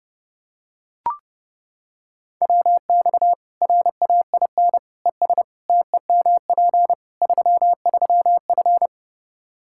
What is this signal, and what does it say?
Morse code, heard as a keyed tone, about 30 wpm, 700 Hz. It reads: WX RAIN ES TEMP 33F
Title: Practice QSO - Exchange 2 - Info - 30wpm